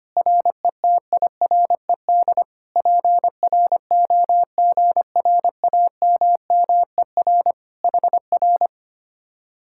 Morse code, heard as a keyed tone, about 24 wpm, 700 Hz. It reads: RETIRED PROGRAMMER HR
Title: Speed-Racing Common QSO Phrases - 25wpm